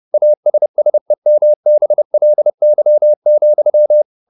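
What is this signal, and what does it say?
Morse code, heard as a keyed tone, about 30 wpm, 600 Hz. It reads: ASSEMBLY,
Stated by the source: Morse code at 30 wpm, the novel Never by Ken Follett in English, from CW med Gnister.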